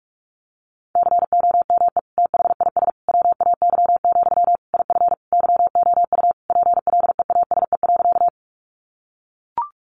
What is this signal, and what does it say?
Morse code, heard as a keyed tone, about 44 wpm, 700 Hz. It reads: COME THIS WAY, IF YOU PLEASE.
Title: Sentences from Top 1500 Words 45wpm